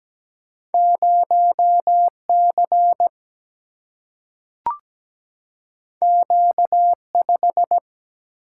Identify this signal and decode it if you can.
Morse code, heard as a keyed tone, about 17 wpm, 700 Hz. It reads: 0C Q5
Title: Single-Letter Single-Number 17wpm